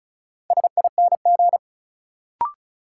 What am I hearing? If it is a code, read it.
Morse code, heard as a keyed tone, about 36 wpm, 700 Hz. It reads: SING
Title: Sentences from Top 400 Words - New Word Drill - 35wpm